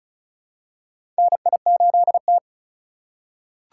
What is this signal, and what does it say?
Morse code, heard as a keyed tone, about 36 wpm, 700 Hz. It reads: NI8T
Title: USA Call Signs - 4 Characters - 2x Character Spacing - 35wpm